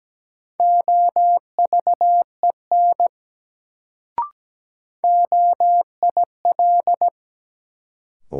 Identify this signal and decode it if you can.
Morse code, heard as a keyed tone, about 17 wpm, 700 Hz. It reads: OVEN OIL
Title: Sentences from Top 1000 Words - Review Part 4 - New Word Drill - 17wpm